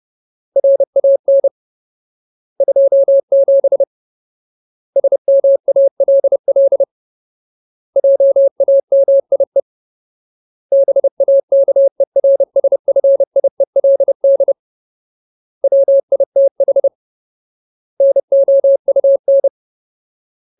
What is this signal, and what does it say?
Morse code, heard as a keyed tone, about 30 wpm, 550 Hz. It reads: RAN 27 SMALL JAMIE BAKERSFIELD WITH NOUN
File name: Ultimate Word List / 30WPM Ultimate